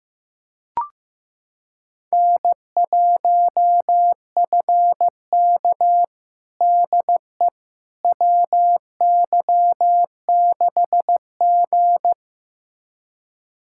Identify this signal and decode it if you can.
Morse code, heard as a keyed tone, about 15 wpm, 700 Hz. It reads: N1FK DE WY6G
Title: USA Callsign DE Callsign - 4 Characters - 15wpm